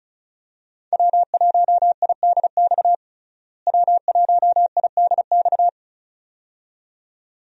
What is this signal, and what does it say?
Morse code, heard as a keyed tone, about 36 wpm, 700 Hz. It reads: W1IDX W1IDX
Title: USA Call Signs - 5 Characters - Repeated 2x - 35wpm